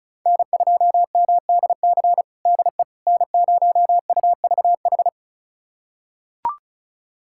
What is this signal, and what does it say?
Morse code, heard as a keyed tone, about 36 wpm, 700 Hz. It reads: N2MDC DE N0UVH
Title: USA Callsign DE Callsign - 5 Characters - 35wpm